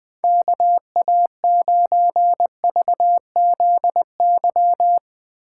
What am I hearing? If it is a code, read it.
Morse code, heard as a keyed tone, about 19 wpm, 700 Hz. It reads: KA9VZY
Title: Speed Racing - USA 2x3 Call Signs 17wpm